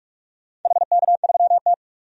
Morse code, heard as a keyed tone, about 44 wpm, 700 Hz. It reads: HK3T
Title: DXCC-100 Call Signs - 4 Characters - 45wpm